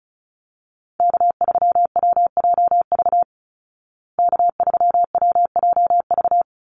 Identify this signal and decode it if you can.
Morse code, heard as a keyed tone, about 34 wpm, 700 Hz. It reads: K3WJV K3WJV
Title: Call Signs Repeated 3x - 35wpm